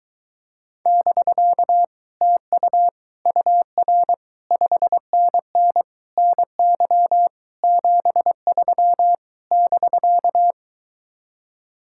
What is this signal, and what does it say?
Morse code, heard as a keyed tone, about 22 wpm, 700 Hz. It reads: <BK> TU UR 5NN NY 73 <BK>